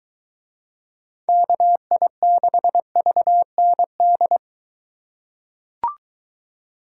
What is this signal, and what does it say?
Morse code, heard as a keyed tone, about 21 wpm, 700 Hz. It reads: KI6VND